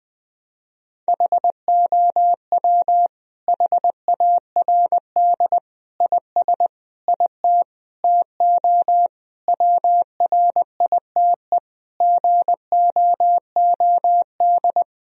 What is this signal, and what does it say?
Morse code, heard as a keyed tone, about 21 wpm, 700 Hz. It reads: HOW HARD IS IT TO WRITE GOOD
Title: Sentences from Top 1000 Words - Review Part 3 - 20wpm